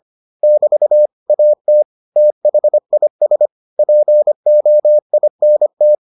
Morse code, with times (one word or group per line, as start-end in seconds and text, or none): 0.4-1.1 <BT>
1.3-1.8 AT
2.2-3.5 THIS
3.8-6.0 POINT